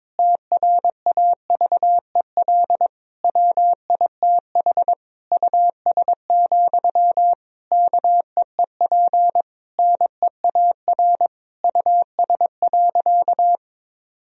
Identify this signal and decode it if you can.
Morse code, heard as a keyed tone, about 22 wpm, 700 Hz. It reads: TRAVEL WITH US, KEEP NEAR US.
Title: Sentences from Top 1000 Words - Review Part 2 - 22wpm